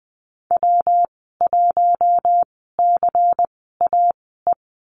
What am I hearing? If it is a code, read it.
Morse code, heard as a keyed tone, about 20 wpm, spaced as for 14 wpm, 700 Hz. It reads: W1CAE
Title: 179 Lesson 29g - Slash - Call Signs 20wpm